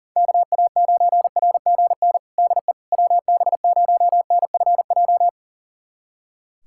Morse code, heard as a keyed tone, about 40 wpm, 700 Hz. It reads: KA9RGN DE WB0NFJ
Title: USA Callsign DE Callsign - 6 Characters - 40wpm